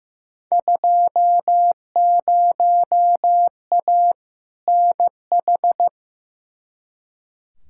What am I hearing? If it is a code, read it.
Morse code, heard as a keyed tone, about 15 wpm, 700 Hz. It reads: 20A NH
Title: Field Day Exchanges Encore - 15wpm